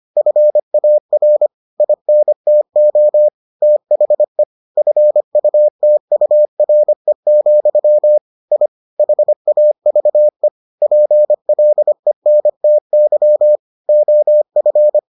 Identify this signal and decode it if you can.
Morse code, heard as a keyed tone, about 25 wpm, 600 Hz. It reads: FAR INTO THE FUTURE, I HAVE PLENTY OF